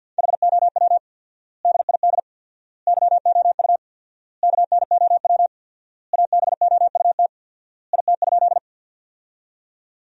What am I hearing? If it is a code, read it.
Morse code, heard as a keyed tone, about 50 wpm, 700 Hz. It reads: HOW DID YOU KNOW ABOUT IT?
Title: 4x Word-Spacing - Sentences from Top 100 Words - 50wpm